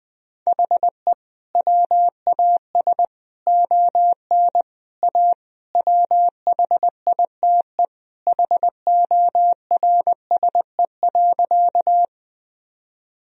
Morse code, heard as a keyed tone, about 20 wpm, 700 Hz. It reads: HE WAS ON A WHITE HORSE.
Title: Sentences from Top 1000 Words - Review Part 1 - 20wpm